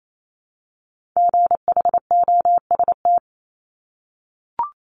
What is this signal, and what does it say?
Morse code, heard as a keyed tone, about 28 wpm, 700 Hz. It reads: GHOST
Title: Multi-letter Phonemes Repeated 3x 28wpm